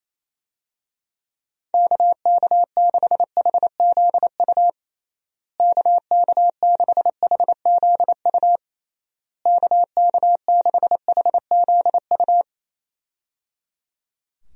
Morse code, heard as a keyed tone, about 28 wpm, 700 Hz. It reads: KK6HZU KK6HZU KK6HZU
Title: USA Call Signs - 6 Characters - Repeated 3x - 28wpm